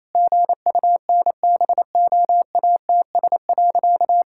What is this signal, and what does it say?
Morse code, heard as a keyed tone, about 28 wpm, 700 Hz. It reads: GUNBOATS.
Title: Pangram Challenge v2 - 28wpm